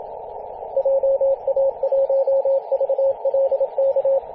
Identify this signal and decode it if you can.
Morse code, heard as a keyed tone, about 27 wpm, 540 Hz, with radio noise. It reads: JA1VLK
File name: posts / 2016-arp-15-2115z-vk9l-g7vjr-18-081mhz-cw